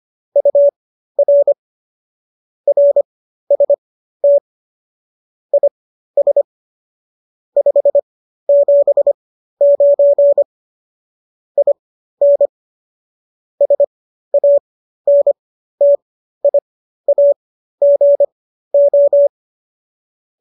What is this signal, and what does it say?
Morse code, heard as a keyed tone, about 25 wpm, spaced as for 13 wpm, 570 Hz. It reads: UR RST IS 579 IN SANTIAGO